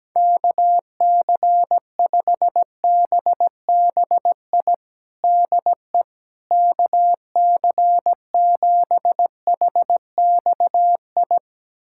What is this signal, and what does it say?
Morse code, heard as a keyed tone, about 17 wpm, 700 Hz. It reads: KC5BBI DE KC7HXI
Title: USA Callsign DE Callsign - 6 Characters - 17wpm